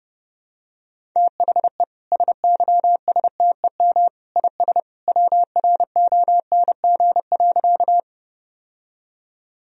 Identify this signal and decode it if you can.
Morse code, heard as a keyed tone, about 30 wpm, 700 Hz. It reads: THE SYSTEM IS WRONG.
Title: Sentences from Top 1000 Words - Review Part 3 - 30wpm